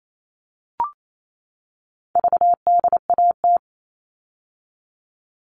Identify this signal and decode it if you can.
Morse code, heard as a keyed tone, about 28 wpm, 700 Hz. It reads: VDAT
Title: All Cut Numbers - 4-Digit 28wpm